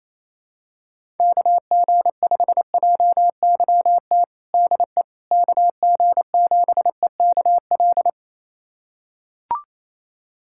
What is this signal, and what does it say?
Morse code, heard as a keyed tone, about 28 wpm, 700 Hz. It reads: KG5JYT DE KG7EKL
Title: USA Callsign DE Callsign 28wpm